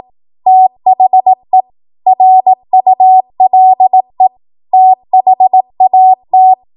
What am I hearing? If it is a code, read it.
Morse code, heard as a keyed tone, about 18 wpm, 750 Hz. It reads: THE RULE THAT
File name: Morse / 18WPM